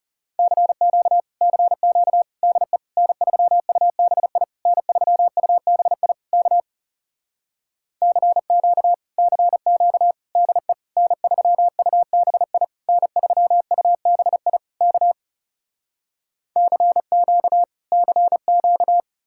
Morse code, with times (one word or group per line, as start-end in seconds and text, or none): 0.4-1.2 CQ
1.4-2.2 CQ
2.4-2.8 DE
3.0-4.4 N3UBI
4.7-6.1 N3UBI
6.3-6.6 K
8.0-8.9 CQ
9.2-10.1 CQ
10.4-10.7 DE
11.0-12.6 N3UBI
12.9-14.6 N3UBI
14.8-15.1 K
16.6-17.6 CQ
17.9-19.0 CQ